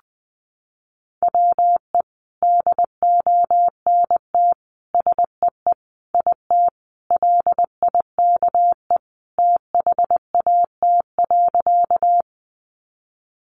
Morse code, heard as a keyed tone, about 20 wpm, 700 Hz. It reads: WE DONT SEE IT LIKE THAT.
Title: Sentences from Top 300 Words 20wpm